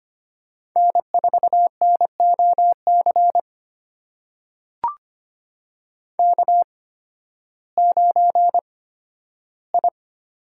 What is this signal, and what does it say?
Morse code, heard as a keyed tone, about 25 wpm, spaced as for 13 wpm, 700 Hz. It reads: N4NOC K9I
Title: USA Call Signs - 5 Characters - 8x Character Spacing - 25wpm